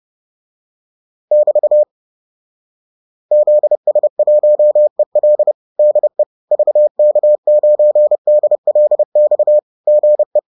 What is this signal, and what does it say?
Morse code, heard as a keyed tone, about 30 wpm, 600 Hz. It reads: <BT> ZS1EL DE VK9DLX GE